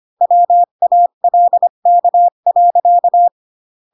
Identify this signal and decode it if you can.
Morse code, heard as a keyed tone, about 25 wpm, 700 Hz. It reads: WALK.
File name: Pinocchio / Pinocchio ch3 20 25 ik1hge